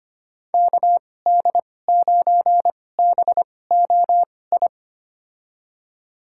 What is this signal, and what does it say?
Morse code, heard as a keyed tone, about 25 wpm, spaced as for 18 wpm, 700 Hz. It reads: KD9BOI
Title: USA Call Signs - 6 Characters - 2x Character Spacing - 25wpm